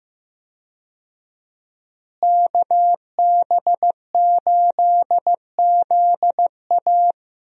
Morse code, heard as a keyed tone, about 16 wpm, 700 Hz. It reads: KB8ZA
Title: USA Call Signs - 5 Characters - Repeated 3x - 15wpm